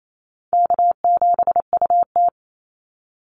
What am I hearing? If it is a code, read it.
Morse code, heard as a keyed tone, about 28 wpm, 700 Hz. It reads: K7UT
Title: Easy Call Signs - 28wpm